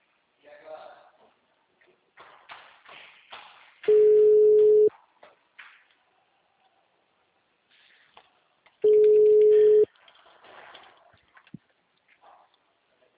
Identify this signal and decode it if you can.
Ringback tone (Europe, 425 Hz), with speech.
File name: out- -2001- - - .105